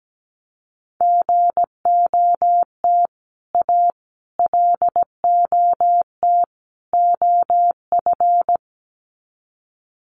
Morse code, heard as a keyed tone, about 17 wpm, 700 Hz. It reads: GOT A LOT OF
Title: Sets of 4 Words 17wpm